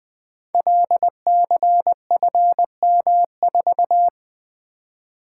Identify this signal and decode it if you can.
Morse code, heard as a keyed tone, about 20 wpm, 700 Hz. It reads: LCFM4
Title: Code Groups - 5 Letter:Numbers - 20wpm